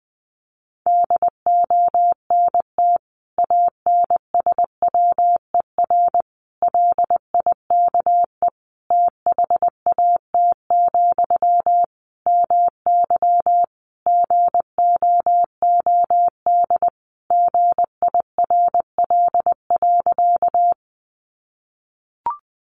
Morse code, heard as a keyed tone, about 20 wpm, 700 Hz. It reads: DONT ANSWER LIKE THAT, MY GOOD GIRL.
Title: Sentences from Top 400 Words 20wpm